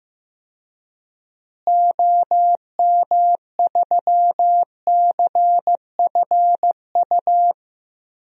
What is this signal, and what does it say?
Morse code, heard as a keyed tone, about 15 wpm, 700 Hz. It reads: OM3CFU
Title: DXCC-100 Call Signs - 6 Characters - Repeated 3x - 15wpm